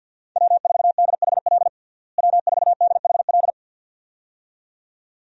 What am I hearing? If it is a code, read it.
Morse code, heard as a keyed tone, about 50 wpm, 700 Hz. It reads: W4DHL W4DHL
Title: USA Call Signs - 5 Characters - Repeated 2x - 50wpm